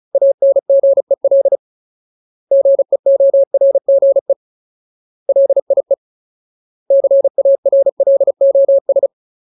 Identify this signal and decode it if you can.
Morse code, heard as a keyed tone, about 36 wpm, 550 Hz. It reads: ANGEL GEORGE LIE CARLOS
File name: Ultimate Word List / 35WPM Ultimate